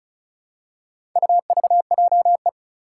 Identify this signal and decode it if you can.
Morse code, heard as a keyed tone, about 36 wpm, 700 Hz. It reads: UVJE
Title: Four Challenging Letter-Number Combinations 35wpm